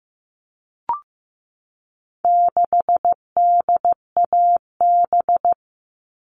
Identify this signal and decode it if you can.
Morse code, heard as a keyed tone, about 15 wpm, 700 Hz. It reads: 6DAB
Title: All Cut Numbers - 4-Digit 15wpm